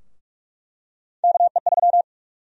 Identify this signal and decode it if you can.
Morse code, heard as a keyed tone, about 44 wpm, 700 Hz. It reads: KE3
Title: USA Call Sign Prefixes - 45wpm